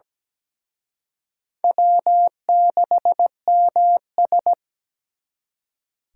Morse code, heard as a keyed tone, about 20 wpm, 700 Hz. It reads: W6MS
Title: Speed Racing - USA Call Signs 17wpm